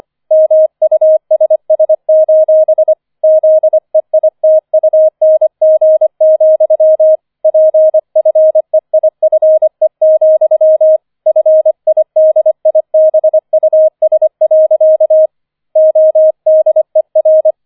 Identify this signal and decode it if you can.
Morse code, heard as a keyed tone, about 25 wpm, 630 Hz. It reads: MUSS: ZEITUNG, PFEIFE, FIDIBUS. ODER
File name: 005 Fünfter Streich 25wpm